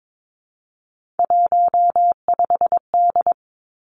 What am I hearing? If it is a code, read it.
Morse code, heard as a keyed tone, about 22 wpm, 700 Hz. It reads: 15D
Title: Three Challenging Letter-Number Combinations 22wpm